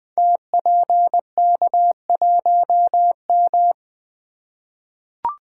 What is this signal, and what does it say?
Morse code, heard as a keyed tone, about 20 wpm, 700 Hz. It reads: TPK1M
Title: Code Groups - 5 Letter:Numbers - 20wpm